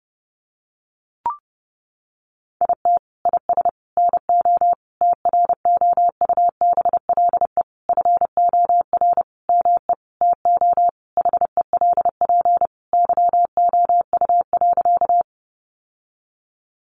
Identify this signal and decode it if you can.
Morse code, heard as a keyed tone, about 30 wpm, 700 Hz. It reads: IT IS NO TROUBLE FOR ME TO HELP YOU.
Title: Sentences from Top 1200 Words 30wpm